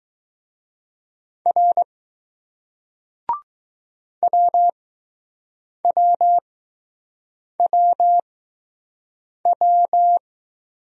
Morse code, heard as a keyed tone, about 20 wpm, spaced as for 6 wpm, 700 Hz. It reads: R WWWW